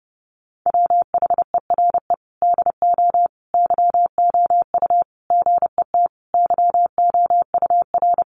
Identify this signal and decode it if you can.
Morse code, heard as a keyed tone, about 30 wpm, 700 Hz. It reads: WHERE DO YOU GET YOUR
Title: Sentences from Top 600 Words 30wpm